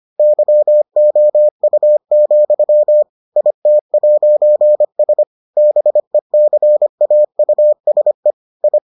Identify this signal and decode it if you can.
Morse code, heard as a keyed tone, about 25 wpm, 600 Hz. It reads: YOU, IT'S BECAUSE I